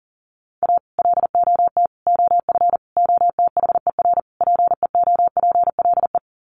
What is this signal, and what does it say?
Morse code, heard as a keyed tone, about 37 wpm, 700 Hz. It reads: A LOT OF OTHER PEOPLE
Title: Sets of 5 Words Encore2 40wpm